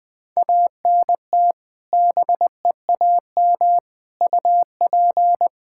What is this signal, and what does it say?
Morse code, heard as a keyed tone, about 20 wpm, 700 Hz. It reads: ANT BEAM UP
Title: K1USN SST QSO Practice 20wpm